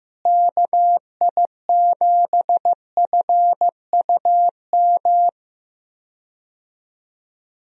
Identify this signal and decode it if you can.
Morse code, heard as a keyed tone, about 15 wpm, 700 Hz. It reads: KI7FUM
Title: USA Call Signs - 6 Characters - Repeated 2x - 15wpm